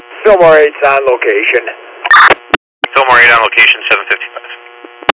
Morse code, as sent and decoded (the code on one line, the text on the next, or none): none